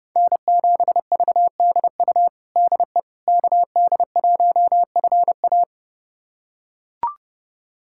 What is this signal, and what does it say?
Morse code, heard as a keyed tone, about 32 wpm, 700 Hz. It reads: N7VDU DE KD1FA
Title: USA Callsign DE Callsign - 5 Characters - 30wpm